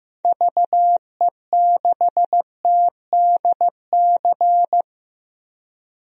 Morse code, heard as a keyed tone, about 16 wpm, 700 Hz. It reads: VE6TDC
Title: DXCC-100 Call Signs - 6 Characters - 15wpm